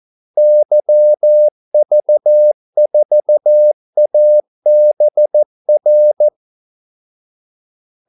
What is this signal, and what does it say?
Morse code, heard as a keyed tone, about 14 wpm, 600 Hz. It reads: YV4ABR
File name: Callsigns-14wpm